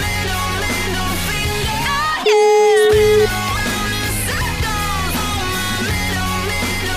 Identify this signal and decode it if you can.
Ringback tone (Europe, 425 Hz), with noise.